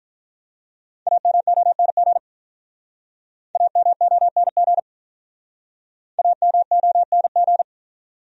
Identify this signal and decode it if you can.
Morse code, heard as a keyed tone, about 43 wpm, 700 Hz. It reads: AMONG AMONG AMONG
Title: Speed-Racing Top 500 Words 35wpm